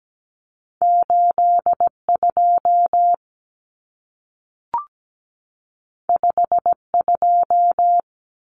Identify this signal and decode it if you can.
Morse code, heard as a keyed tone, about 17 wpm, 700 Hz. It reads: 82 52
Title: Common Cut Numbers - 2-Digit 17wpm